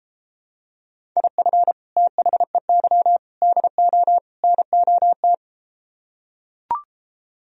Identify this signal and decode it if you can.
Morse code, heard as a keyed tone, about 33 wpm, 700 Hz. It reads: IF THEY DO NOT